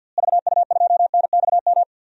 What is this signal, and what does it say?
Morse code, heard as a keyed tone, about 46 wpm, 700 Hz. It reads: VU2NXM
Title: DXCC-100 Call Signs - 6 Characters - Repeated 3x - 50wpm